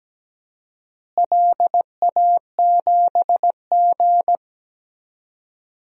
Morse code, heard as a keyed tone, about 17 wpm, 700 Hz. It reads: LA7G
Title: DXCC-100 Call Signs - 4 Characters - Repeated 3x - 17wpm